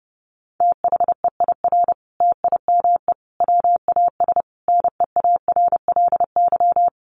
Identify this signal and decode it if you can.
Morse code, heard as a keyed tone, about 28 wpm, 700 Hz. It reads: THEIR TIME WAS NEARLY
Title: Sentences from Top 1300 Words 30wpm